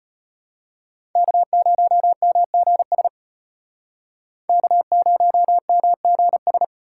Morse code, heard as a keyed tone, about 32 wpm, 700 Hz. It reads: K0MGS K0MGS